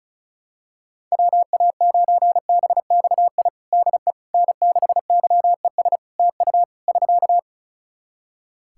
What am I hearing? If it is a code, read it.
Morse code, heard as a keyed tone, about 36 wpm, 700 Hz. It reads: WA9BXI DE N6YES TU <SK>
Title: Practice QSO - Exchange 4 - Ending - 35wpm